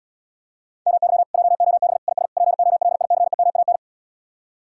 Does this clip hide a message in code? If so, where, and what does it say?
Yes: Morse code from 0.9 to 3.8 s, heard as a keyed tone, about 77 wpm, 700 Hz. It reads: MY JOB IS POLICEMAN